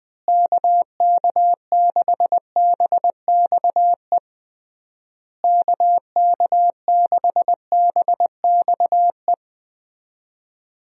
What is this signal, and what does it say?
Morse code, heard as a keyed tone, about 21 wpm, 700 Hz. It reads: KK6BXE KK6BXE
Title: USA Call Signs - 6 Characters - Repeated 3x - 20wpm